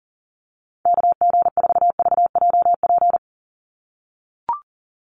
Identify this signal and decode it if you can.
Morse code, heard as a keyed tone, about 37 wpm, 700 Hz. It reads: KG4VJP
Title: USA Call Signs 40wpm